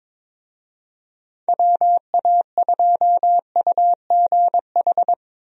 Morse code, heard as a keyed tone, about 22 wpm, 700 Hz. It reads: WA2UGH
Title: USA Call Signs - 6 Characters - Repeated 2x - 22wpm